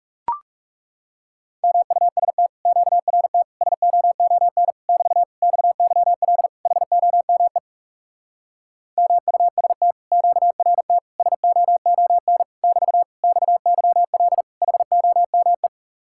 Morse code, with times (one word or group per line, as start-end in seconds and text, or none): 1.6-2.5 MUST
2.7-3.4 QRT
3.6-4.7 SOON
4.9-5.2 <BT>
5.4-6.5 XYL
6.7-7.6 HOME
9.0-9.9 MUST
10.1-11.0 QRT
11.2-12.4 SOON
12.6-13.0 <BT>
13.2-14.4 XYL
14.6-15.7 HOME